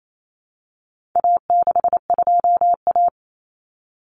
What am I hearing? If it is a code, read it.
Morse code, heard as a keyed tone, about 28 wpm, 700 Hz. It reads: A62A
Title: Common Cut Numbers - 4-Digit 28wpm